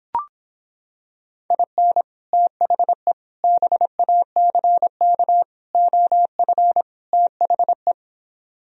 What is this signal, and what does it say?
Morse code, heard as a keyed tone, about 26 wpm, 700 Hz. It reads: IN THE BACK OF THE